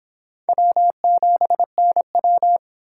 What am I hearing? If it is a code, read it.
Morse code, heard as a keyed tone, about 26 wpm, 700 Hz. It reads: W7NW